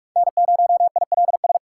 Morse code, heard as a keyed tone, about 41 wpm, 700 Hz. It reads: N0ILS
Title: USA Call Signs - 5 Characters - 4x Character Spacing - 45wpm